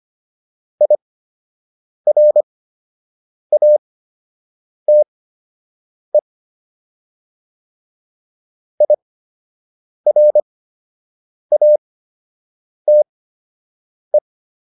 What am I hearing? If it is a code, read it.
Morse code, heard as a keyed tone, about 25 wpm, spaced as for 7 wpm, 600 Hz. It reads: IRATE IRATE